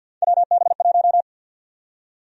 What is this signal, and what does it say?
Morse code, heard as a keyed tone, about 53 wpm, 700 Hz. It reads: WB1
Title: USA Call Sign Prefixes - 50wpm